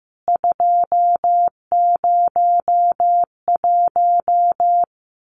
Morse code, heard as a keyed tone, about 15 wpm, 700 Hz. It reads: 201
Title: CWT Exchange 15wpm